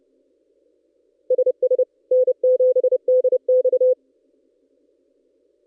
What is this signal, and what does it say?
Morse code, heard as a keyed tone, about 29 wpm, 510 Hz. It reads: SS N7DX
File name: contests / ss-cw-k5tr-op-n5rz -1700